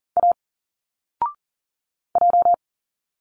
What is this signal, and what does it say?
Morse code, heard as a keyed tone, about 41 wpm, 700 Hz. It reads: A J
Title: Single Letters 40wpm